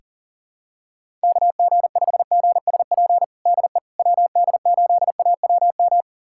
Morse code, heard as a keyed tone, about 40 wpm, 700 Hz. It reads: KG5GSP DE WD8AWM
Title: USA Callsign DE Callsign 40wpm